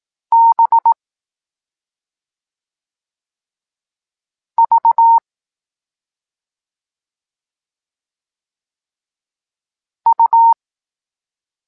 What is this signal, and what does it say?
Morse code, heard as a keyed tone, about 18 wpm, 920 Hz. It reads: B V U
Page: Tekst 14 – doskonalenie odbioru poznanych liter